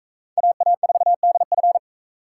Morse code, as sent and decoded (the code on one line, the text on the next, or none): .- .- ....- -.. ..-.
AA4DF